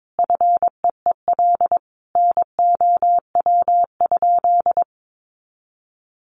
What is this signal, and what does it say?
Morse code, heard as a keyed tone, about 22 wpm, 700 Hz. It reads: FEEL NOW?
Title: Sentences from Top 400 Words 22wpm